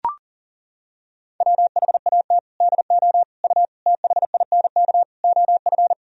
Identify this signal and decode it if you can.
Morse code, heard as a keyed tone, about 40 wpm, 700 Hz. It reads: WHAT DO U THINK OF
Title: Sentences from Top 200 Words and Abbreviations 40wpm